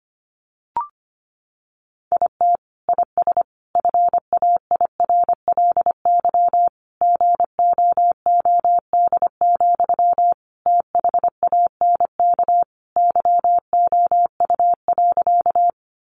Morse code, heard as a keyed tone, about 25 wpm, 700 Hz. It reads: IT IS FAIRLY GOOD, THANK YOU.
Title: Sentences from Top 1100 Words 25wpm